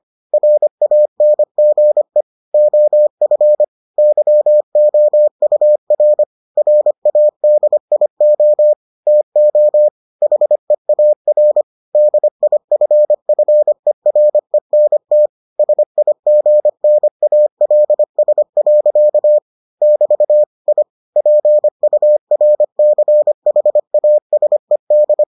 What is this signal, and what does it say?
Morse code, heard as a keyed tone, about 25 wpm, 600 Hz. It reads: RANGE OF YOUR RADIO TO HEAR DIFFERENT SIGNALS. <BT> I PURCHASED